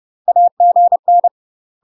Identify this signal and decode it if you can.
Morse code, heard as a keyed tone, about 30 wpm, 700 Hz. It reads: AGN
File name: UK MASTER QSOs TRAINING / 30WPM / UK MASTER QSOs 0029